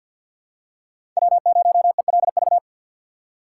Morse code, heard as a keyed tone, about 51 wpm, 700 Hz. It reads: W0ELV
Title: Code Groups - 5 Letter:Numbers - 50wpm